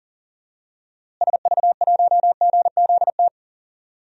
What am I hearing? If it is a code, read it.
Morse code, heard as a keyed tone, about 37 wpm, 700 Hz. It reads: SV1GZT
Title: DXCC-100 Call Signs - 6 Characters - Repeated 3x - 40wpm